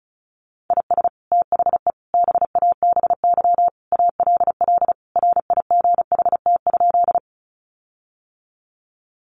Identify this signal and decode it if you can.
Morse code, heard as a keyed tone, about 34 wpm, 700 Hz. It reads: IS THE BABY ALL RIGHT?
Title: Sentences from Top 1300 Words 35wpm